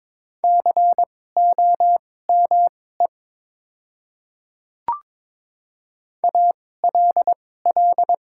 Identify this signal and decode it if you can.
Morse code, heard as a keyed tone, about 22 wpm, spaced as for 16 wpm, 700 Hz. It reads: COME ALL
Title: 2x Character Spacing - Top 200 Words - 22wpm